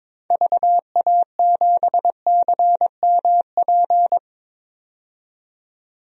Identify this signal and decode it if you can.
Morse code, heard as a keyed tone, about 23 wpm, 700 Hz. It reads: VA7CMP
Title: DXCC-100 Call Signs - 6 Characters - 22wpm